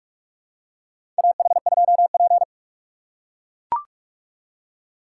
Morse code, heard as a keyed tone, about 42 wpm, 700 Hz. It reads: AH2P
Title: USA Call Signs - 4 Characters - 3x Character Spacing - 45wpm